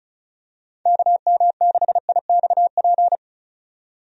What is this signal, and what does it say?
Morse code, heard as a keyed tone, about 34 wpm, 700 Hz. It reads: KM6IXP
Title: USA Call Signs - 6 Characters - 35wpm